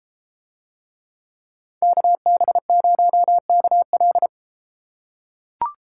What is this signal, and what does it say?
Morse code, heard as a keyed tone, about 32 wpm, 700 Hz. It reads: KB0KL